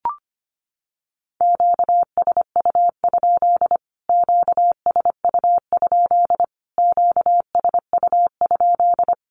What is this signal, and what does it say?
Morse code, heard as a keyed tone, about 25 wpm, 700 Hz. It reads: QSU? QSU? QSU?
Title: Q-Codes Repeated 3x 25wpm